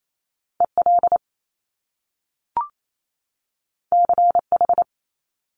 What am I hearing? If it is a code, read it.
Morse code, heard as a keyed tone, about 28 wpm, 700 Hz. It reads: EL CH
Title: Top Two Letter Combinations in Words 28wpm